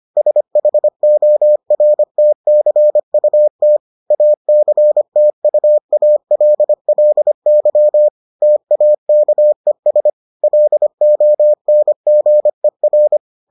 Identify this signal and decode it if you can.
Morse code, heard as a keyed tone, about 25 wpm, 600 Hz. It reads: SHORTCUT ACTUALLY TAKES LONGER